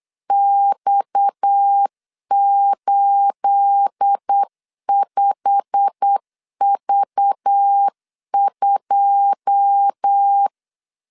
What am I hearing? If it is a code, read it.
Morse code, heard as a keyed tone, about 8 wpm, 800 Hz. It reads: X85V2
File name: LEZIONI / 9 28 8wpm 3512